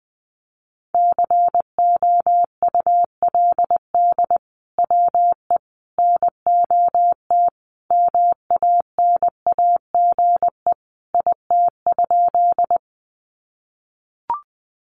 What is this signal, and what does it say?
Morse code, heard as a keyed tone, about 20 wpm, 700 Hz. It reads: COULD WE NOT MANAGE IT?
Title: Sentences from Top 1400 Words 20wpm